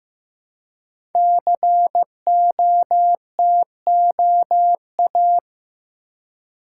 Morse code, heard as a keyed tone, about 15 wpm, 700 Hz. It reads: COTOA
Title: Code Groups - 5 Letters - 15wpm